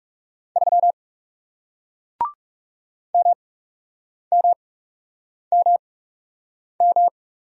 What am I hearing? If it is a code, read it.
Morse code, heard as a keyed tone, about 40 wpm, spaced as for 8 wpm, 700 Hz. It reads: 3 MMMM